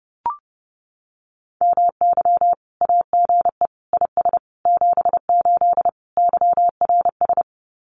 Morse code, heard as a keyed tone, about 30 wpm, 700 Hz. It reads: MY AGE IS 78 YRS
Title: Common QSO Phrases Encore 30wpm